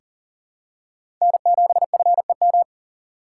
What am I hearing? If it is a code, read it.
Morse code, heard as a keyed tone, about 40 wpm, 700 Hz. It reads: N7FEM
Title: USA Call Signs - 5 Characters - 3x Character Spacing - 40wpm